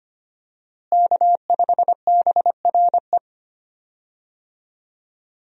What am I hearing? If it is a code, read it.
Morse code, heard as a keyed tone, about 25 wpm, 700 Hz. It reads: K5BRE
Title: USA Call Signs - 5 Characters - Repeated 3x - 25wpm